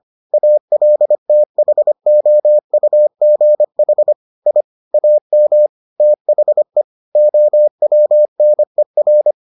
Morse code, heard as a keyed tone, about 25 wpm, 600 Hz. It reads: ALTHOUGH I AM THE OWNER